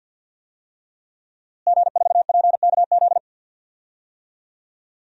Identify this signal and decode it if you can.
Morse code, heard as a keyed tone, about 50 wpm, 700 Hz. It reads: G4PKZ
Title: DXCC-100 Call Signs - 5 Characters - Repeated 3x - 50wpm